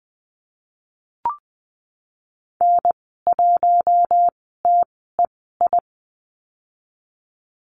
Morse code, heard as a keyed tone, about 20 wpm, spaced as for 15 wpm, 700 Hz. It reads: N1TEI
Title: 042 Lesson 8g - Number 1 - Call Signs 20wpm